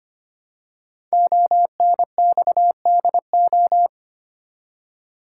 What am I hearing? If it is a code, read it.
Morse code, heard as a keyed tone, about 25 wpm, 700 Hz. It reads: ONXDO
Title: Code Groups - 5 Letters - 25wpm